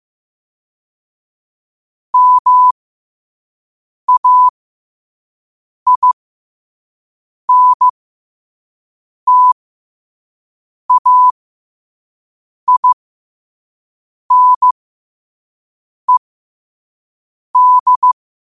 Morse code, heard as a keyed tone, about 15 wpm, spaced as for 5 wpm, 1000 Hz. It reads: MAINTAINED
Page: New quotes every day in morse code at 5 Words per minute.